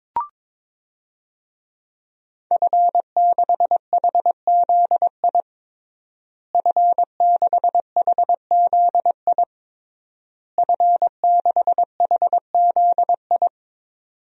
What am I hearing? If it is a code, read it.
Morse code, heard as a keyed tone, about 22 wpm, 700 Hz. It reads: F6HZI F6HZI F6HZI
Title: DXCC-100 Call Signs - 5 Characters - Repeated 3x - 22wpm